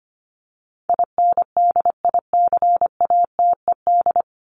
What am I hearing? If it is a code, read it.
Morse code, heard as a keyed tone, about 25 wpm, 700 Hz. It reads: INDICATED
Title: ED Suffix - Top 100 Words with Base Words 25wpm